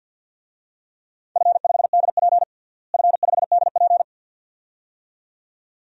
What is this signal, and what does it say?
Morse code, heard as a keyed tone, about 52 wpm, 700 Hz. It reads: F5DP F5DP
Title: DXCC-100 Call Signs - 4 Characters - Repeated 2x - 50wpm